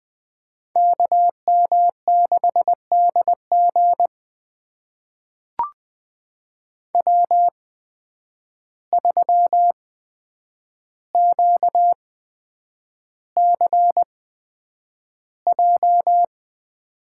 Morse code, heard as a keyed tone, about 20 wpm, spaced as for 9 wpm, 700 Hz. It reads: KM6DG W3QCJ
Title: USA Call Signs - 8x Character Spacing - 20wpm